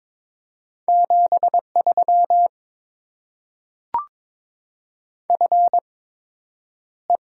Morse code, heard as a keyed tone, about 22 wpm, 700 Hz. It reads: 7 3 F E
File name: Top 20 QSO Elements - 8x Character Spacing - 22wpm